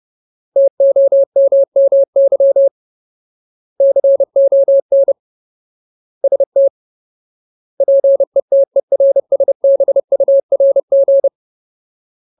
Morse code, heard as a keyed tone, about 30 wpm, 550 Hz. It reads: TOMMY CON ST PETERSBURG
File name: Ultimate Word List / 30WPM Ultimate